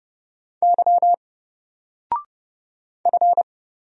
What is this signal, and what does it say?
Morse code, heard as a keyed tone, about 30 wpm, 700 Hz. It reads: Y F
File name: Single-Letter-Number-30wpm